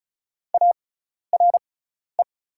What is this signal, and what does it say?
Morse code, heard as a keyed tone, about 34 wpm, spaced as for 12 wpm, 700 Hz. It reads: ARE
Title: 6x Character Spacing - Top 300 Words - 35wpm